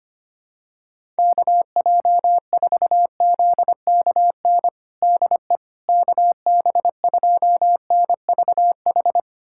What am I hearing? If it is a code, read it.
Morse code, heard as a keyed tone, about 25 wpm, 700 Hz. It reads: KJ4ZKN DE KB2NVH
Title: USA Callsign DE Callsign - 6 Characters - 25wpm